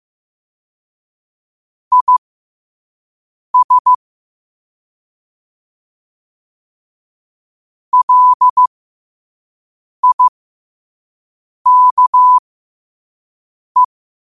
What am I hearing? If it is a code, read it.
Morse code, heard as a keyed tone, about 15 wpm, spaced as for 5 wpm, 1000 Hz. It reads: IS LIKE
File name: qotdfiles / 5wpm